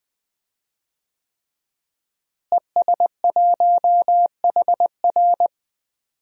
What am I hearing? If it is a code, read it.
Morse code, heard as a keyed tone, about 20 wpm, 700 Hz. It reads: ES1HR
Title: DXCC-100 Call Signs - 5 Characters - Repeated 3x - 20wpm